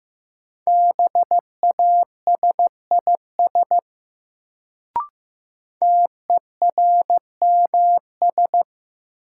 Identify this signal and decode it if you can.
Morse code, heard as a keyed tone, about 15 wpm, 700 Hz. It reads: BASIS TERMS
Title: Sets of 3 Words - New Word Drill - 15wpm